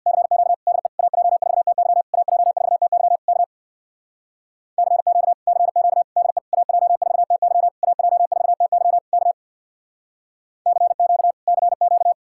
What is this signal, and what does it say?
Morse code, heard as a keyed tone, about 62 wpm, 700 Hz. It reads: CQ DE AJ4TY AJ4TY K CQ CQ DE AJ4TY AJ4TY K CQ CQ
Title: Speed-Racing Common QSO Phrases - 50wpm